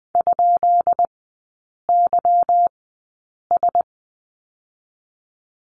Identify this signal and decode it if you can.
Morse code, heard as a keyed tone, about 20 wpm, spaced as for 8 wpm, 690 Hz. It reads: ?YS
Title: 145 Lesson 24e - Letter y - Three Characters 20wpm